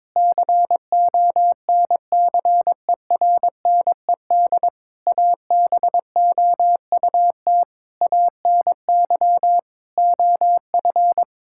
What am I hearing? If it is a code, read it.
Morse code, heard as a keyed tone, about 22 wpm, 700 Hz. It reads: CONCERNED ABOUT ANY OF
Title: Sentences from Top 1000 Words - Review Part 6 - 22wpm